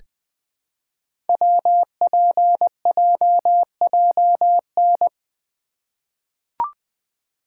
Morse code, heard as a keyed tone, about 20 wpm, 700 Hz. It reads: WPJJN
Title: Code Groups - 5 Letters - 20wpm